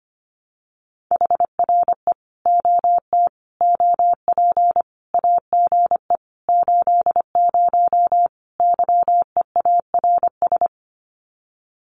Morse code, heard as a keyed tone, about 25 wpm, 700 Hz. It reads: HRE OT OP AGE 80 YEARS
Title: Common QSO Phrases 25wpm